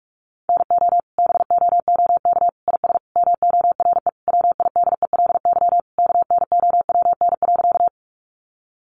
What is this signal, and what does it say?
Morse code, heard as a keyed tone, about 44 wpm, 700 Hz. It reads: NO BOOK IS MORE WIDELY KNOWN.
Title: Sentences from Top 1200 Words 45wpm